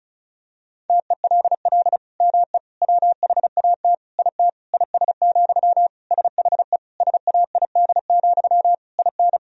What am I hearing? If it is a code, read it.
Morse code, heard as a keyed tone, about 36 wpm, 700 Hz. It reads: TELL ME WHAT IT IS, SHE SAID, IN
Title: Sentences from Top 1000 Words - Review Part 1 - 35wpm